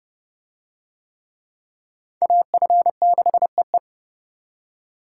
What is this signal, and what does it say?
Morse code, heard as a keyed tone, about 27 wpm, 700 Hz. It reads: AF6EE